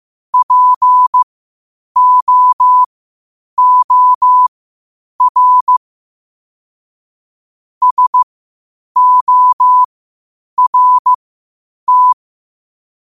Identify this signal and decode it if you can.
Morse code, heard as a keyed tone, about 15 wpm, spaced as for 8 wpm, 1000 Hz. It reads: POOR SORT
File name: qotdfiles / 8wpm